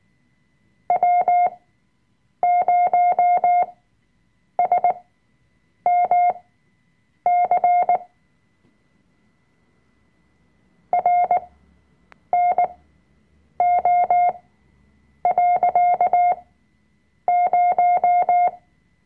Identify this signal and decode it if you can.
Morse code, heard as a keyed tone, about 19 wpm, spaced as for 7 wpm, 710 Hz. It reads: W0SMC RNO.0
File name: Workshop Dateien / CW-Online-Kurs / Lektion 005